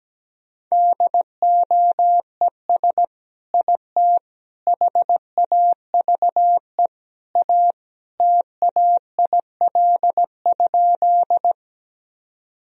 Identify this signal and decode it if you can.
Morse code, heard as a keyed tone, about 17 wpm, 700 Hz. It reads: DOES IT HAVE A TAIL?
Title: Sentences from Top 1000 Words - Review Part 3 - 17wpm